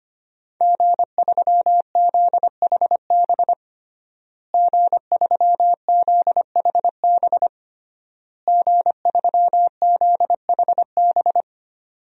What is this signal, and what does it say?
Morse code, heard as a keyed tone, about 25 wpm, 700 Hz. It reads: G3ZHB G3ZHB G3ZHB
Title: DXCC-100 Call Signs - 5 Characters - Repeated 3x - 25wpm